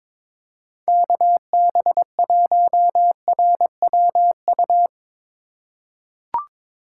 Morse code, heard as a keyed tone, about 21 wpm, 700 Hz. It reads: KB1RWU
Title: USA Call Signs - 6 Characters - 3x Character Spacing - 22wpm